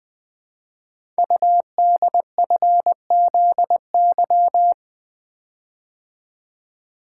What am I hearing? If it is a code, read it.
Morse code, heard as a keyed tone, about 20 wpm, 700 Hz. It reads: UDFZY
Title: Code Groups - 5 Letters - 20wpm